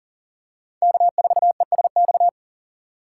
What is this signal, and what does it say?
Morse code, heard as a keyed tone, about 40 wpm, 700 Hz. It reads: K4ESX
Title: USA Call Signs - 3x Character Spacing - 40wpm